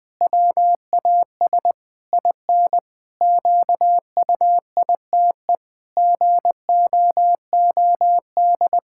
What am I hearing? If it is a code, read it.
Morse code, heard as a keyed tone, about 20 wpm, 700 Hz. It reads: WAS IN QUITE GOOD
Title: Pangram Challenge v2 - 20wpm